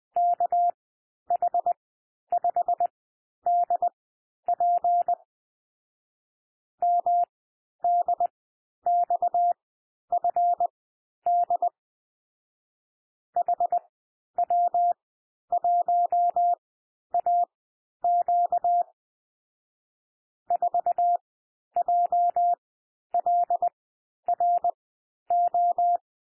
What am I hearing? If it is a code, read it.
Morse code, heard as a keyed tone, about 20 wpm, spaced as for 11 wpm, 700 Hz. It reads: KH5DP MDXFD HW1AQ 4JLRO
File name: cwmp3 / M1 SOUND MIXED 20 wpm